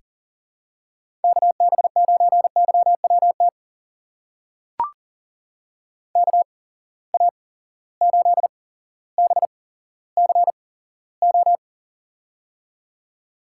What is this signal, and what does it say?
Morse code, heard as a keyed tone, about 40 wpm, spaced as for 17 wpm, 700 Hz. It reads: KB9YWT KA8BCO
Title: USA Call Signs - 6 Characters - 8x Character Spacing - 40wpm